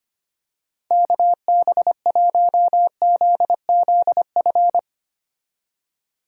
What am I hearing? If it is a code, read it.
Morse code, heard as a keyed tone, about 25 wpm, 700 Hz. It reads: KB1ZZF
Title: USA Call Signs - 6 Characters - 25wpm